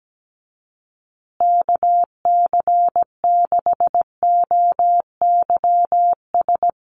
Morse code, heard as a keyed tone, about 17 wpm, 700 Hz. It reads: KC6OYS
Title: USA Call Signs Repeated 2x 17wpm